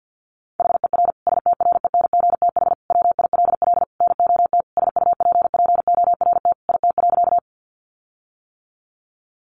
Morse code, heard as a keyed tone, about 50 wpm, 700 Hz. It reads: HER STRENGTH WILL NOT SUPPORT IT.
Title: Sentences from Top 1300 Words 50wpm